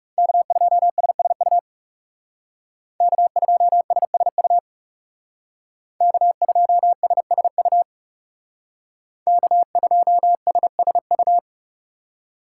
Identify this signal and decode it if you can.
Morse code, heard as a keyed tone, about 35 wpm, 700 Hz. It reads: K2SSU K2SSU K2SSU K2SSU
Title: Speed Racing - USA 1x3 Call Signs 30wpm